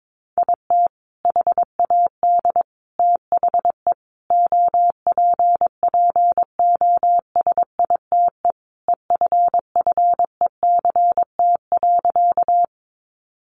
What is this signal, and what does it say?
Morse code, heard as a keyed tone, about 22 wpm, 700 Hz. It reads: IT HAD THE OPPOSITE EFFECT.
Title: Sentences from Top 1300 Words 22wpm